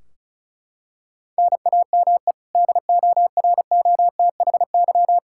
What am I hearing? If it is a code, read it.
Morse code, heard as a keyed tone, about 34 wpm, 700 Hz. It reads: NAME DOROTHY
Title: Practice QSO - Exchange 1 - Beginning - 35wpm